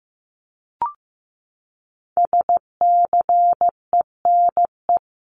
Morse code, heard as a keyed tone, about 15 wpm, 700 Hz. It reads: SCENE
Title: Top 1001-1250 Words 15wpm